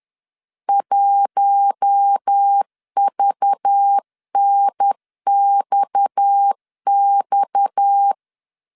Morse code, heard as a keyed tone, about 10 wpm, 800 Hz. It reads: 1VNXX
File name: LEZIONI / 6 PKX0 / 6 PKX 10wpm 2426